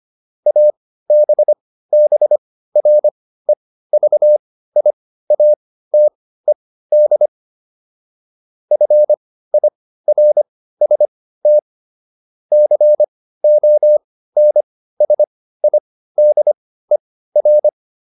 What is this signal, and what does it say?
Morse code, heard as a keyed tone, about 25 wpm, spaced as for 15 wpm, 600 Hz. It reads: ABBREVIATED FIRST CONSIDER